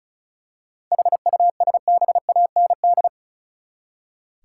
Morse code, heard as a keyed tone, about 36 wpm, 700 Hz. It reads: HUSBAND
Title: Seven Letter Words - 35wpm